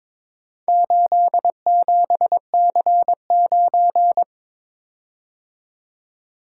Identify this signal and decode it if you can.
Morse code, heard as a keyed tone, about 22 wpm, 700 Hz. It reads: 87C9
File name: practice / 22wpm / Four Letter-Number Combinations - 22wpm